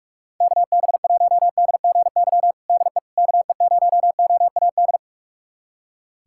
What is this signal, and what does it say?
Morse code, heard as a keyed tone, about 46 wpm, 700 Hz. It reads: KB1DGY DE KE0OAD
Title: USA Callsign DE Callsign - 6 Characters - 45wpm